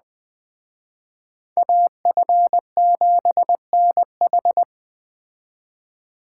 Morse code, heard as a keyed tone, about 20 wpm, 700 Hz. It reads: AF7NH
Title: Speed Racing - USA 2x2 Call Signs 17wpm